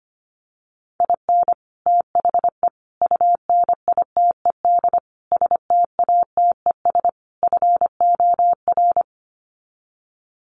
Morse code, heard as a keyed tone, about 25 wpm, 700 Hz. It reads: IN THE UNITED STATES FOR
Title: Sets of 5 Words Encore2 25wpm